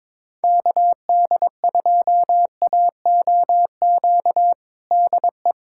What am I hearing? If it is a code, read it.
Morse code, heard as a keyed tone, about 22 wpm, 700 Hz. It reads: KD2AOQ DE
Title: USA Callsign DE Callsign - 6 Characters - 22wpm